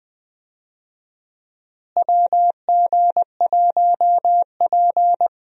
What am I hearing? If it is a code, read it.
Morse code, heard as a keyed tone, about 20 wpm, 700 Hz. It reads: WG1P
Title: USA Call Signs - 4 Characters - Repeated 2x - 20wpm